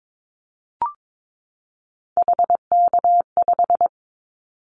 Morse code, heard as a keyed tone, about 22 wpm, 700 Hz. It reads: HK5
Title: Callsign Prefixes for Top 100 DXCC - 22wpm